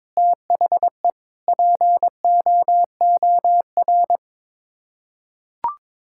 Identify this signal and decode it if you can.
Morse code, heard as a keyed tone, about 22 wpm, 700 Hz. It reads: THE POOR
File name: practice / 22wpm / Sets of 8 Words - 22wpm